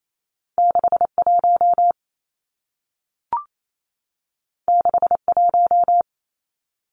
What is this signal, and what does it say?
Morse code, heard as a keyed tone, about 28 wpm, 700 Hz. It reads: 61 61
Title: Numbers - 2-Digit 28wpm